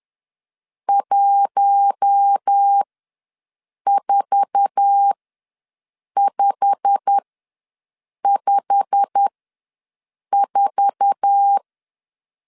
Morse code, heard as a keyed tone, about 10 wpm, spaced as for 6 wpm, 800 Hz. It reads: 14554
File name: LEZIONI / 8 YJ45 / 8 YJ45 ws 10wpm 3843